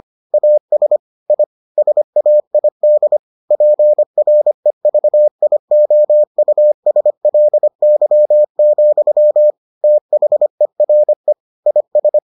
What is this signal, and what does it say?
Morse code, heard as a keyed tone, about 25 wpm, 600 Hz. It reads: AS I SAID PREVIOUSLY, THERE IS